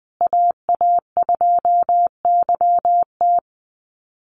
Morse code, heard as a keyed tone, about 20 wpm, 700 Hz. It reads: AA2YT
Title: USA Call Signs Repeated 3x 20wpm